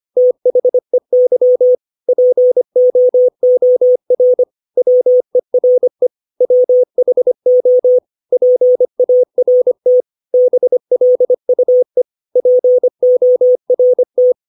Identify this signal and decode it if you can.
Morse code, heard as a keyed tone, about 25 wpm, 500 Hz. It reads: THEY POOR WERE WHO PART BLUE PORT
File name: 632 words / 632 Words 25WPM